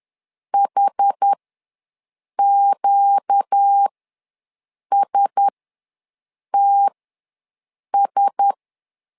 Morse code, heard as a keyed tone, about 10 wpm, spaced as for 6 wpm, 800 Hz. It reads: HQSTS
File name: LEZIONI / 4 Q1S ws 10wpm 2208